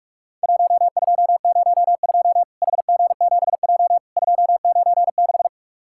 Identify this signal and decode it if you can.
Morse code, heard as a keyed tone, about 44 wpm, 700 Hz. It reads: 1202 HG7J 296
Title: Worked All Europe DX Contest - QTC Exchange - 45wpm